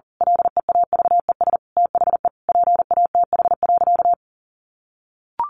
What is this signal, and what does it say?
Morse code, heard as a keyed tone, about 40 wpm, 700 Hz. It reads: LEAVES THE PATH.
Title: A Sound Of Thunder 40wpm